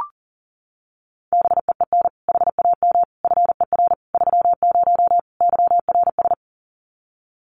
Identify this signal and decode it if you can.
Morse code, heard as a keyed tone, about 37 wpm, 700 Hz. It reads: BEEN HAM FER 30 YRS
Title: Common QSO Phrases Encore 40wpm